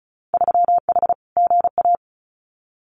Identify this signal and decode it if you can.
Morse code, heard as a keyed tone, about 34 wpm, 700 Hz. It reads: 3H GA
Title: Winter Field Day Exchanges Encore 35wpm